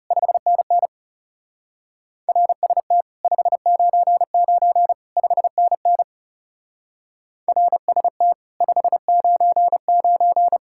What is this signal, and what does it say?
Morse code, heard as a keyed tone, about 36 wpm, 700 Hz. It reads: 5NN RST 599 5NN RST 599
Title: Speed-Racing Common QSO Phrases - 30wpm